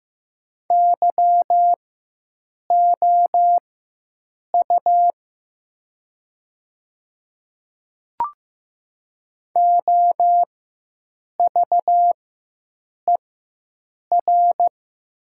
Morse code, heard as a keyed tone, about 15 wpm, spaced as for 7 wpm, 700 Hz. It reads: YOU OVER
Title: 4x Character Spacing - Top 100 Words - 15wpm